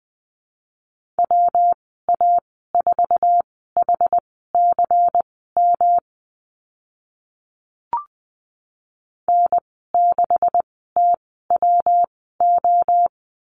Morse code, heard as a keyed tone, about 20 wpm, spaced as for 14 wpm, 700 Hz. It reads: WA4HCM N6TWO
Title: 161 Lesson 26g - Letter g - Call Signs 20wpm